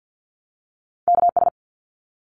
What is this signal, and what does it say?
Morse code, heard as a keyed tone, about 50 wpm, 700 Hz. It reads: KS
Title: ARRL Sections 50wpm